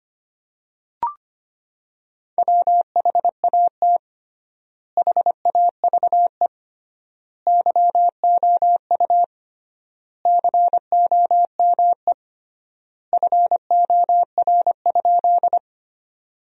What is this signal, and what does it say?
Morse code, heard as a keyed tone, about 25 wpm, 700 Hz. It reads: WHAT HAVE YOU COME FOR?
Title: 3x Word-Spacing - Sentences from Top 100 Words - 25wpm